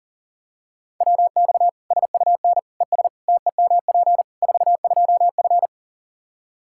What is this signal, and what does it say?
Morse code, heard as a keyed tone, about 40 wpm, 700 Hz. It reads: WX SUN ES TEMP 42F
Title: K1USN SST QSO Practice 40wpm